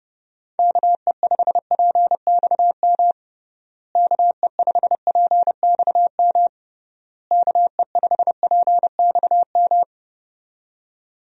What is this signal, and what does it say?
Morse code, heard as a keyed tone, about 30 wpm, 700 Hz. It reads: KE5PXM KE5PXM KE5PXM
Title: USA Call Signs - 6 Characters - Repeated 3x - 30wpm